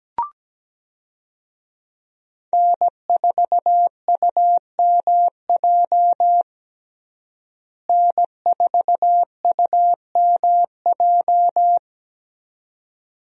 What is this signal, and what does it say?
Morse code, heard as a keyed tone, about 17 wpm, 700 Hz. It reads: N4UMJ N4UMJ
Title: USA Call Signs - 5 Characters - Repeated 2x - 17wpm